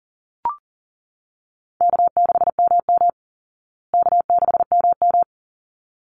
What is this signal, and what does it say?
Morse code, heard as a keyed tone, about 39 wpm, 700 Hz. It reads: K6MM K6MM
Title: Easy Call Signs Repeated 2x - 40wpm